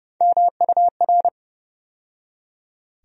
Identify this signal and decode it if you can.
Morse code, heard as a keyed tone, about 30 wpm, 700 Hz. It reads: MUR
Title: USA Call Sign Postfixes - 3 Letters 30wpm